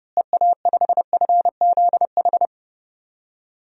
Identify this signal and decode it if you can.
Morse code, heard as a keyed tone, about 32 wpm, 700 Hz. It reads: EA5FZH
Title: DXCC-100 Call Signs - 6 Characters - Repeated 2x - 30wpm